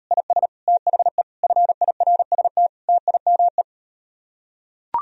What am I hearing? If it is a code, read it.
Morse code, heard as a keyed tone, about 39 wpm, 700 Hz. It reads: IS THE FIRST TIME